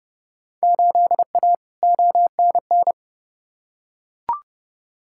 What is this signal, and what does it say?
Morse code, heard as a keyed tone, about 30 wpm, 700 Hz. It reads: 8A ONN
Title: Field Day Exchanges Encore - 30wpm